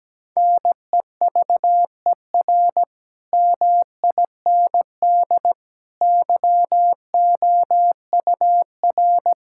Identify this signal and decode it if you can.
Morse code, heard as a keyed tone, about 17 wpm, 700 Hz. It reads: NEVER MIND YOUR
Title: Sentences from Top 1600 Words 17wpm